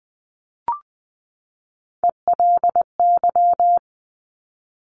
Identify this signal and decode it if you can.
Morse code, heard as a keyed tone, about 20 wpm, 700 Hz. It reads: ELY
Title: Top Three Letter Combinations in Words 20wpm